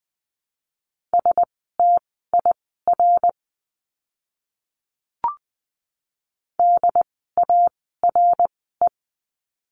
Morse code, heard as a keyed tone, about 20 wpm, spaced as for 15 wpm, 700 Hz. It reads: STIR DARE
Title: 068 Lesson 12f - Letter d - Words 20wpm